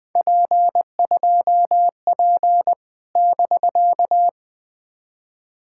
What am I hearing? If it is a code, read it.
Morse code, heard as a keyed tone, about 20 wpm, 700 Hz. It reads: P2P <BK>
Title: POTA QSO Practice - 20wpm